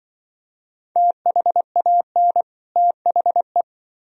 Morse code, heard as a keyed tone, about 24 wpm, 700 Hz. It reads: THAN THE
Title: Speed-Racing - Sets of 2 Words 20wpm